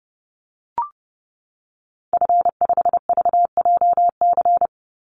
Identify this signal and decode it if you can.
Morse code, heard as a keyed tone, about 30 wpm, 700 Hz. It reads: F5VJC
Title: Call Signs - 30wpm